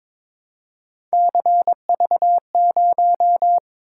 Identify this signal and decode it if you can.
Morse code, heard as a keyed tone, about 22 wpm, 700 Hz. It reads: CV0
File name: practice / 22wpm / Three Letter-Number Combinations - 22wpm